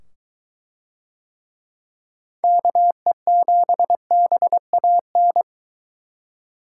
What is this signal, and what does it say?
Morse code, heard as a keyed tone, about 21 wpm, 700 Hz. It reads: KE7BAN